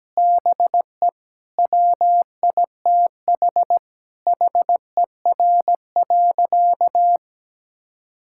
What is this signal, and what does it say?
Morse code, heard as a keyed tone, about 17 wpm, 700 Hz. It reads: BE WITH HER.
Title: Sentences from Top 1000 Words - Review Part 6 - 17wpm